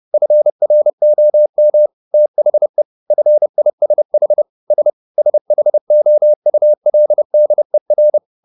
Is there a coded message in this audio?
Yes: Morse code from 0.1 to 8.2 s, heard as a keyed tone, about 30 wpm, 600 Hz. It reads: FROM THE FISH S SHOULDER